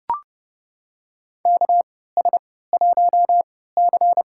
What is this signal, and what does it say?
Morse code, heard as a keyed tone, about 30 wpm, spaced as for 17 wpm, 700 Hz. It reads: KS1C
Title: USA Call Signs - 3x Character Spacing - 30wpm